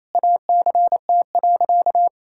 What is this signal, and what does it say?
Morse code, heard as a keyed tone, about 28 wpm, 700 Hz. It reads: ACT.
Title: Sentences from Top 1000 Words - Review Part 6 - 28wpm